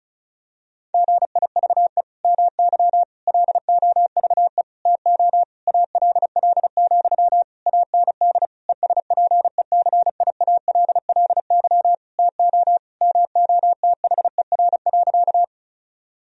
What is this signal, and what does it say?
Morse code, heard as a keyed tone, about 36 wpm, 700 Hz. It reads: GIVE MY LOVE TO ALL, AND ESPECIALLY TO MOTHER.
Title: Sentences from Top 1000 Words - Review Part 4 - 35wpm